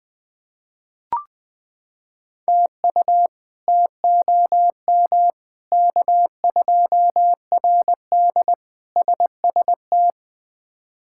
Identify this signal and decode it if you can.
Morse code, heard as a keyed tone, about 20 wpm, 700 Hz. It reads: TU TOM K2RD SST
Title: K1USN SST QSO Practice - 20wpm